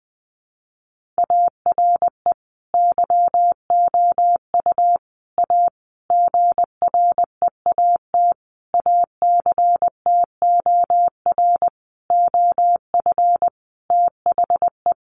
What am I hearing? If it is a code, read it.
Morse code, heard as a keyed tone, about 20 wpm, 700 Hz. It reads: ARE YOU A GREAT ACTOR OF THE
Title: Sentences from Top 1500 Words 20wpm